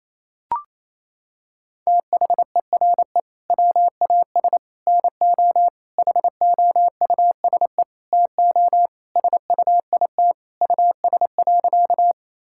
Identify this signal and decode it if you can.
Morse code, heard as a keyed tone, about 28 wpm, 700 Hz. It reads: THERE WAS NO HOUSE TO SUIT US.
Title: Sentences from Top 1600 Words 28wpm